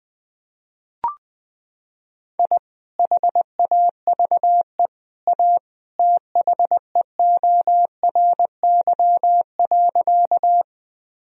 Morse code, heard as a keyed tone, about 21 wpm, 700 Hz. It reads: I HAVE A THEORY.
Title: Sentences from Top 1000 Words - Review Part 3 - 20wpm